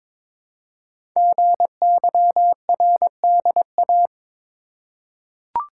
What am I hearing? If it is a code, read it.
Morse code, heard as a keyed tone, about 22 wpm, 700 Hz. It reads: GYRDA
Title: Code Groups - 5 Letters - 22wpm